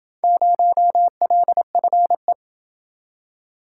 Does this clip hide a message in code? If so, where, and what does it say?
Yes: Morse code from 0.2 to 2.3 s, heard as a keyed tone, about 27 wpm, 700 Hz. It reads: 0LFE